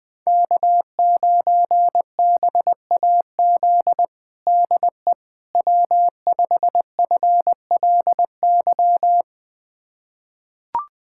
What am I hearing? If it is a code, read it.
Morse code, heard as a keyed tone, about 20 wpm, 700 Hz. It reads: K9BAZ DE W5FLY
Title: USA Callsign DE Callsign - 5 Characters - 20wpm